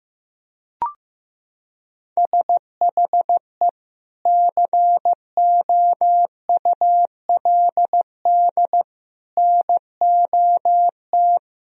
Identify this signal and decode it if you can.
Morse code, heard as a keyed tone, about 15 wpm, 700 Hz. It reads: SHE COULD NOT
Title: Sentences from Top 1000 Words - Review Part 3 - 15wpm